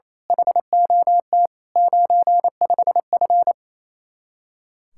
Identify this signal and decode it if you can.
Morse code, heard as a keyed tone, about 27 wpm, 700 Hz. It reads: HOT 95F
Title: Speed-Racing Common QSO Phrases Encore - 28wpm